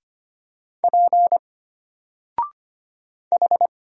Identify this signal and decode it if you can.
Morse code, heard as a keyed tone, about 25 wpm, 700 Hz. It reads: P H
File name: Single-Letter-Number-25wpm